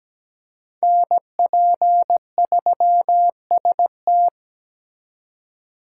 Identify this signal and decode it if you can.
Morse code, heard as a keyed tone, about 21 wpm, 700 Hz. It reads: NP3ST
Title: Speed Racing - USA 2x2 Call Signs 17wpm